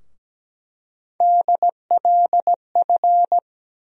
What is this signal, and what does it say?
Morse code, heard as a keyed tone, about 17 wpm, 700 Hz. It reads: DLF
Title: USA Call Sign Postfixes - 3 Letters 17wpm